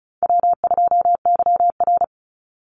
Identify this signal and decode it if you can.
Morse code, heard as a keyed tone, about 34 wpm, 700 Hz. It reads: W2YR
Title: Easy Call Signs Repeated 2x - 35wpm